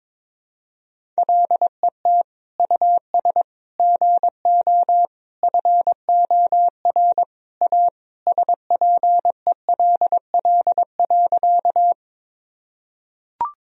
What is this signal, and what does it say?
Morse code, heard as a keyed tone, about 22 wpm, 700 Hz. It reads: LET US GO FOR A SPELL.
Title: Sentences from Top 1000 Words - Review Part 1 - 22wpm